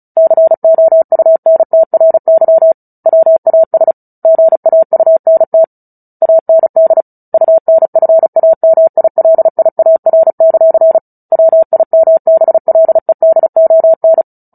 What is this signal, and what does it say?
Morse code, heard as a keyed tone, about 34 wpm, 650 Hz. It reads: COUNTRY WAS GAUNT AND UNFAMILIAR; WIMBLEDON
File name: Book Two / 09-Chapter 9 - 35 WPM